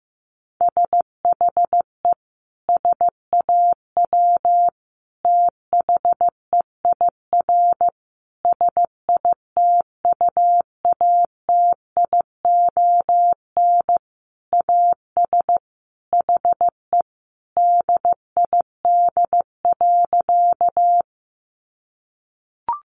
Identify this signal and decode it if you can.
Morse code, heard as a keyed tone, about 15 wpm, 700 Hz. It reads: SHE SAW THEIR SITUATION AS HE DID.
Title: Sentences from Top 1200 Words 15wpm